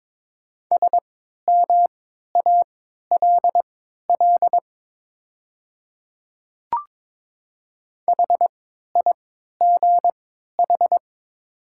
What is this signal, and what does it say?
Morse code, heard as a keyed tone, about 22 wpm, spaced as for 12 wpm, 700 Hz. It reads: SMALL HIGH
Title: 3x Character Spacing - Top 500 Words - 22wpm